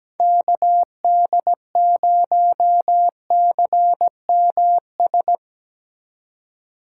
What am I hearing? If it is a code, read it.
Morse code, heard as a keyed tone, about 19 wpm, 700 Hz. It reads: KD0CMS
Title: Speed Racing - USA Call Signs 17wpm